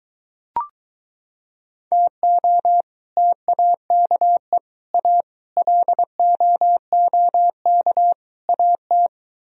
Morse code, heard as a keyed tone, about 24 wpm, 700 Hz. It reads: TO TAKE A LOOK AT